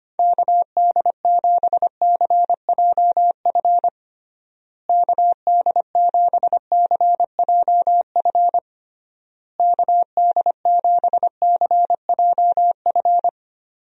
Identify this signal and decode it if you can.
Morse code, heard as a keyed tone, about 25 wpm, 700 Hz. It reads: KD7CJF KD7CJF KD7CJF
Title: USA Call Signs - 6 Characters - Repeated 3x - 25wpm